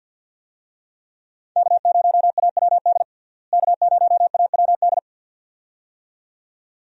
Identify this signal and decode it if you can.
Morse code, heard as a keyed tone, about 51 wpm, 700 Hz. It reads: K0AWD K0AWD
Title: USA Call Signs - 5 Characters - Repeated 2x - 50wpm